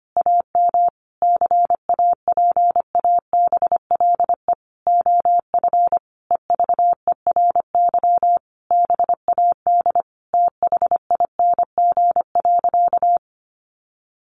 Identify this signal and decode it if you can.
Morse code, heard as a keyed tone, about 25 wpm, 700 Hz. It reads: AM CAPABLE OF EVERY BAD THING.
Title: Sentences from Top 1200 Words 25wpm